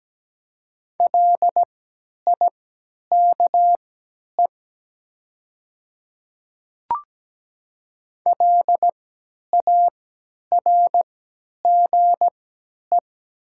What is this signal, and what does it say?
Morse code, heard as a keyed tone, about 17 wpm, spaced as for 10 wpm, 700 Hz. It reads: LIKE LARGE
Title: 3x Character Spacing - Top 200 Words - 17wpm